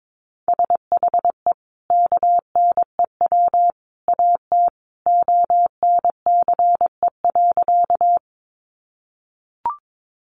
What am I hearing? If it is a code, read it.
Morse code, heard as a keyed tone, about 22 wpm, 700 Hz. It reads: SHE KNEW AT ONCE.
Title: Sentences from Top 400 Words 22wpm